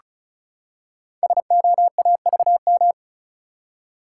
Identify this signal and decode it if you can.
Morse code, heard as a keyed tone, about 36 wpm, 690 Hz. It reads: SOAVM
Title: Code Groups - 5 Letters - 35wpm